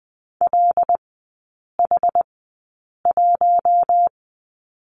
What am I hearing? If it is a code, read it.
Morse code, heard as a keyed tone, about 20 wpm, spaced as for 8 wpm, 700 Hz. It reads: LH1
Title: 074 Lesson 13e - Letter L - Three Characters 20wpm